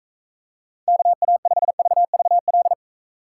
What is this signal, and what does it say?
Morse code, heard as a keyed tone, about 42 wpm, 700 Hz. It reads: KA5VVL